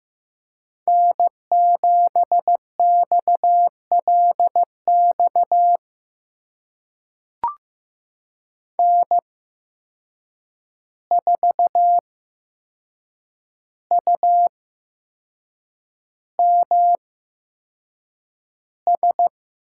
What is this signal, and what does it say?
Morse code, heard as a keyed tone, about 15 wpm, spaced as for 6 wpm, 700 Hz. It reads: N7XLX N4UMS
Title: USA Call Signs - 5 Characters - 8x Character Spacing - 15wpm